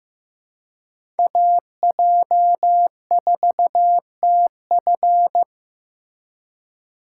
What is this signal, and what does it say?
Morse code, heard as a keyed tone, about 15 wpm, 700 Hz. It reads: AJ4TF
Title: K1USN SST QSO Practice - 15wpm